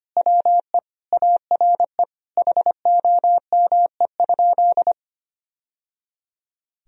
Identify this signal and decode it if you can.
Morse code, heard as a keyed tone, about 25 wpm, 700 Hz. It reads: WE ARE HOME?
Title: Sentences from Top 1000 Words - Review Part 1 - 25wpm